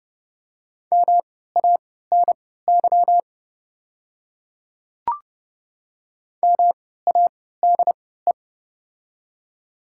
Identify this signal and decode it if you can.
Morse code, heard as a keyed tone, about 30 wpm, spaced as for 17 wpm, 700 Hz. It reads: MANY MADE
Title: 3x Character Spacing - Top 200 Words - 30wpm